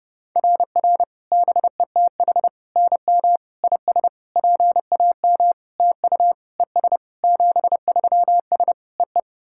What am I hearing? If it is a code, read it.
Morse code, heard as a keyed tone, about 30 wpm, 690 Hz. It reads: RR BETH NM IS PAM TU ES 73S EE
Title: SOTA QSO Practice 30wpm